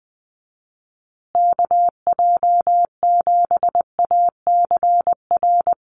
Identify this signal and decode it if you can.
Morse code, heard as a keyed tone, about 20 wpm, 700 Hz. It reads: KJ7ACR
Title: USA Call Signs Repeated 2x 20wpm